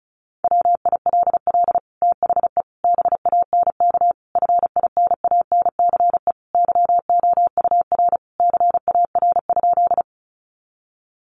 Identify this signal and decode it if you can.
Morse code, heard as a keyed tone, about 34 wpm, 700 Hz. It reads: WILL THE BANK FINANCE YOUR CAR?
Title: Sentences from Top 1500 Words 35wpm